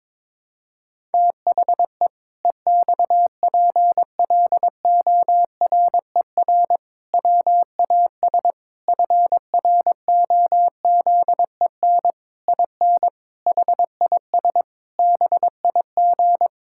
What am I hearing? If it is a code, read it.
Morse code, heard as a keyed tone, about 22 wpm, 700 Hz. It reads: THE EXPLORER WAS FROZEN IN HIS BIG
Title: Pangram Challenge v2 - 22wpm